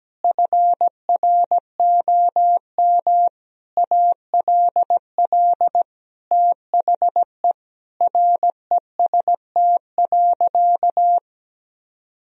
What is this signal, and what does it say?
Morse code, heard as a keyed tone, about 17 wpm, 700 Hz. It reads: FROM ALL THE REST.
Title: Sentences from Top 1000 Words - Review Part 1 - 17wpm